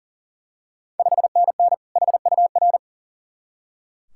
Morse code, heard as a keyed tone, about 40 wpm, 690 Hz. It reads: 5NN HUR
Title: Ontario QSO Party Exchange 40wpm